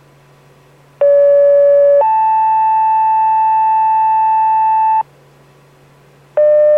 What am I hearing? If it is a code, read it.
Two-tone page, heard as a one-second tone, then a three-second tone, about 570 then 880 Hz. It is heard from 1.0 to 5.0 s.